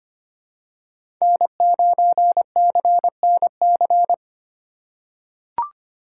Morse code, heard as a keyed tone, about 25 wpm, 700 Hz. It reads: N9CNC
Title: USA Call Signs - 2x Character Spacing - 25wpm